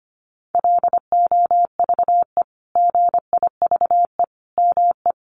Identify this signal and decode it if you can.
Morse code, heard as a keyed tone, about 25 wpm, 700 Hz. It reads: LOVE GIVE ME
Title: Sentences from Top 1300 Words 25wpm